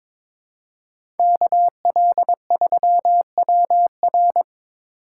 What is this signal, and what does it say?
Morse code, heard as a keyed tone, about 22 wpm, 700 Hz. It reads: KL3WR
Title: USA Call Signs - 5 Characters - 8x Character Spacing - 22wpm